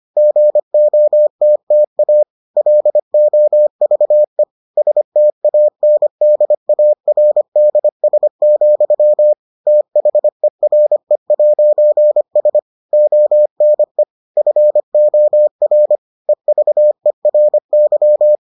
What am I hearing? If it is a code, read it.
Morse code, heard as a keyed tone, about 25 wpm, 600 Hz. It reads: GOTTA LOVE STANDARDS, THERE'S ONE FOR EVERY